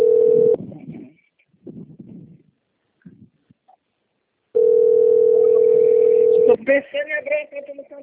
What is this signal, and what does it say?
Ringback tone (North America), 440 + 480 Hz, with speech.
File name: out- - - - - .20362